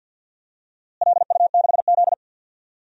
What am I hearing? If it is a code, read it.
Morse code, heard as a keyed tone, about 46 wpm, 700 Hz. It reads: LU6Z
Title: DXCC-100 Call Signs - 4 Characters - Repeated 2x - 50wpm